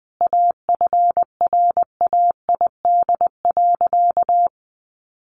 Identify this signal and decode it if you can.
Morse code, heard as a keyed tone, about 20 wpm, 700 Hz. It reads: AFRAID.
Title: Sentences from Top 1500 Words 20wpm